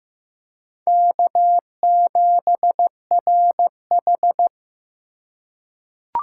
Morse code, heard as a keyed tone, about 15 wpm, 700 Hz. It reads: K7RH
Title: USA Call Signs - 4 Characters - 3x Character Spacing - 15wpm